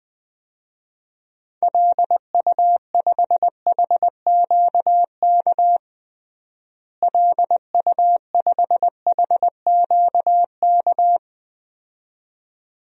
Morse code, heard as a keyed tone, about 20 wpm, 700 Hz. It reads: LU5HQK LU5HQK
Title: DXCC-100 Call Signs - 6 Characters - Repeated 2x - 20wpm